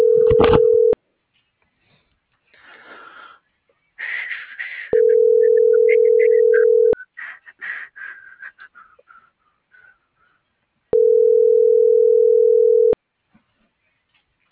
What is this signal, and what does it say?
Ringback tone (North America), 440 + 480 Hz.